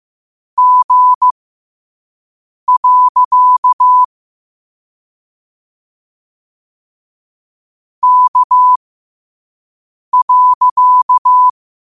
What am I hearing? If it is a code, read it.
Morse code, heard as a keyed tone, about 15 wpm, spaced as for 5 wpm, 1000 Hz. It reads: G. K.